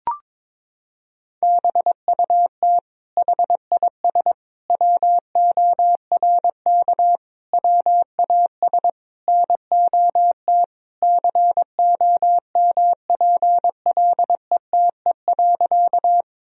Morse code, heard as a keyed tone, about 22 wpm, 700 Hz. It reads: BUT HIS WORK WAS NOT COMPLETE.
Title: Sentences from Top 1000 Words - Review Part 2 - 22wpm